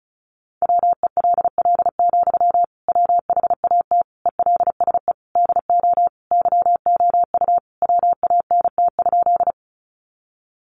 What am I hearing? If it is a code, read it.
Morse code, heard as a keyed tone, about 34 wpm, 700 Hz. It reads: WELL, WHAT ELSE DO YOU WANT?
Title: Sentences from Top 800 Words 35wpm